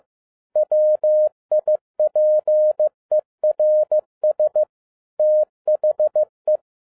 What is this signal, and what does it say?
Morse code, heard as a keyed tone, about 15 wpm, 610 Hz. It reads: WIPERS THE
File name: 2012-01-mp3 / n07-ARA JR0DIL-15wpm